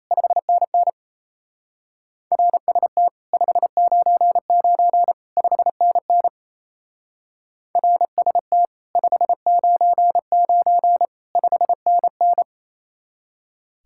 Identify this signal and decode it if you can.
Morse code, heard as a keyed tone, about 34 wpm, 700 Hz. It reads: 5NN RST 599 5NN RST 599 5NN
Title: Speed-Racing Common QSO Phrases Encore - 28wpm